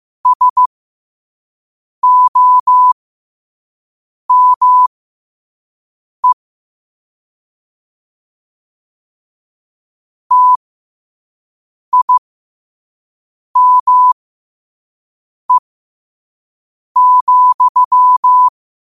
Morse code, heard as a keyed tone, about 15 wpm, spaced as for 5 wpm, 1000 Hz. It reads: SOME TIME,
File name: qotdfiles / 5wpm